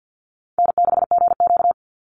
Morse code, heard as a keyed tone, about 50 wpm, 700 Hz. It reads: N6GQ
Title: Easy Call Signs - 50wpm